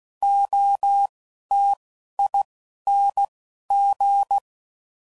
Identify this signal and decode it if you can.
Morse code, heard as a keyed tone, about 16 wpm, 800 Hz. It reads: O T I N G